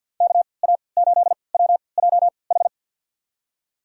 Morse code, heard as a keyed tone, about 52 wpm, spaced as for 29 wpm, 700 Hz. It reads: KA8WJH
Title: USA Call Signs - 6 Characters - 3x Character Spacing - 50wpm